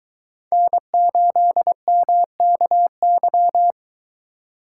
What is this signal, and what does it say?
Morse code, heard as a keyed tone, about 22 wpm, 700 Hz. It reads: N8MKY